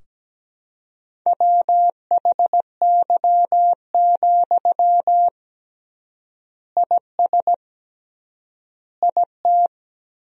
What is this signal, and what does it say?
Morse code, heard as a keyed tone, about 17 wpm, 700 Hz. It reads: WHY, IS IT
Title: 3x Word-Spacing - Sentences from Top 300 Words - 17wpm